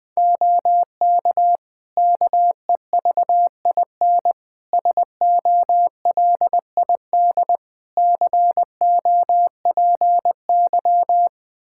Morse code, heard as a keyed tone, about 20 wpm, 700 Hz. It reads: OK KEVIN SOLID COPY
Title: Practice QSO - Exchange 3 - More Info - 20wpm